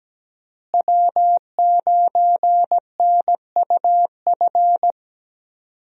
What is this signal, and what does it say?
Morse code, heard as a keyed tone, about 17 wpm, 700 Hz. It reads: W9NUF
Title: USA Call Signs - 5 Characters - Repeated 3x - 17wpm